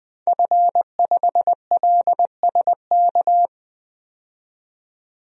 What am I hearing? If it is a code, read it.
Morse code, heard as a keyed tone, about 20 wpm, 700 Hz. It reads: F5LSK
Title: DXCC-100 Call Signs - 5 Characters - 20wpm